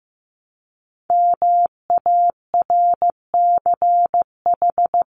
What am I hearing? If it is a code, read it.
Morse code, heard as a keyed tone, about 15 wpm, 700 Hz. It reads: MARCH
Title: Top 1501-1750 Words 15wpm